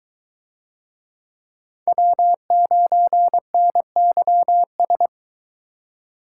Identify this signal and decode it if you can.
Morse code, heard as a keyed tone, about 24 wpm, 700 Hz. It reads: W9NYS